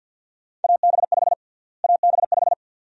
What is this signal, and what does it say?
Morse code, heard as a keyed tone, about 51 wpm, 700 Hz. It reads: AB5 AB5
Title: USA Call Sign Prefixes - Repeated 2x - 50wpm